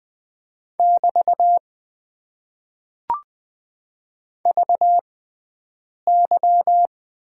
Morse code, heard as a keyed tone, about 20 wpm, spaced as for 7 wpm, 700 Hz. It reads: <BT> VY
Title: Top 90 QSO Elements - 6x Character Spacing - 20wpm